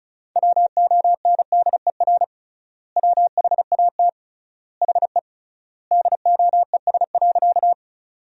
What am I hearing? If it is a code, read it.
Morse code, heard as a keyed tone, about 36 wpm, 700 Hz. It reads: WONDER WHAT HE DOES.
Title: 3x Word-Spacing - Sentences from Top 500 Words - 35wpm